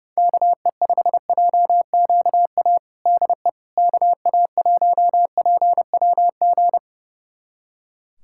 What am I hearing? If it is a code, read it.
Morse code, heard as a keyed tone, about 30 wpm, 700 Hz. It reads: KE5JQA DE KA1PWG
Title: USA Callsign DE Callsign 30wpm